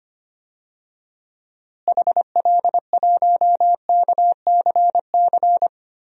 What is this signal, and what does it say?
Morse code, heard as a keyed tone, about 25 wpm, 700 Hz. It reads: HL1KCC
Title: DXCC-100 Call Signs - 6 Characters - Repeated 3x - 25wpm